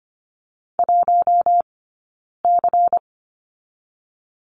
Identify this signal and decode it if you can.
Morse code, heard as a keyed tone, about 25 wpm, 700 Hz. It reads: 1 C
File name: course / 25:10wpm / 092 Lesson 16d - Letter c - Two Characters 25wpm